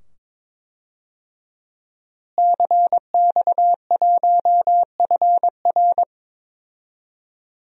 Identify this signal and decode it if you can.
Morse code, heard as a keyed tone, about 22 wpm, 700 Hz. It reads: CX1FR
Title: DXCC-100 Call Signs - 5 Characters - Repeated 2x - 22wpm